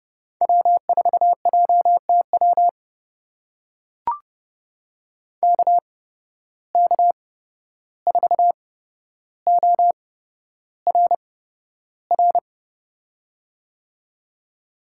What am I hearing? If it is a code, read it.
Morse code, heard as a keyed tone, about 32 wpm, spaced as for 12 wpm, 700 Hz. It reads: W4JTW KK4ORR
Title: USA Call Signs - 8x Character Spacing - 30wpm